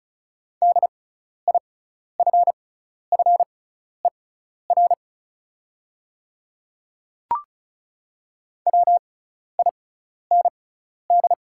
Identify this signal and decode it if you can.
Morse code, heard as a keyed tone, about 36 wpm, spaced as for 12 wpm, 690 Hz. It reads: DIFFER WIND
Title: 6x Character Spacing - Top 500 Words - 35wpm